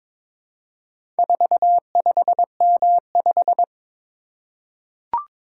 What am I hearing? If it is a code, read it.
Morse code, heard as a keyed tone, about 22 wpm, 700 Hz. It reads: 45M5
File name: practice / 22wpm / Four Letter-Number Combinations - 22wpm